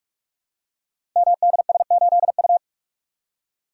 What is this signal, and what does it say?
Morse code, heard as a keyed tone, about 44 wpm, 700 Hz. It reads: MDS8U
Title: Code Groups - 5 Letter:Numbers - 45wpm